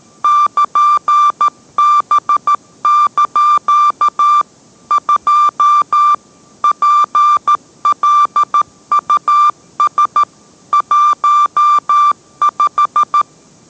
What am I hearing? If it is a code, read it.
Morse code, heard as a keyed tone, about 13 wpm, 1180 Hz. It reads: <KN>B) 2 PLUS 15